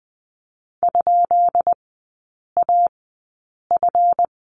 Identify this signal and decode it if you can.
Morse code, heard as a keyed tone, about 19 wpm, spaced as for 8 wpm, 700 Hz. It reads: ?AF
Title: 138 Lesson 23e - Letter f - Three Characters 20wpm